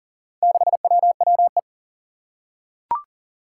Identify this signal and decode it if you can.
Morse code, heard as a keyed tone, about 40 wpm, 700 Hz. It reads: 6WWE
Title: Four Challenging Letter-Number Combinations 40wpm